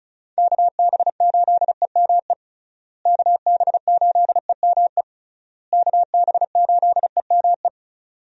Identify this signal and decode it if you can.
Morse code, heard as a keyed tone, about 36 wpm, 700 Hz. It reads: KB8EME KB8EME KB8EME
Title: USA Call Signs - 6 Characters - Repeated 3x - 35wpm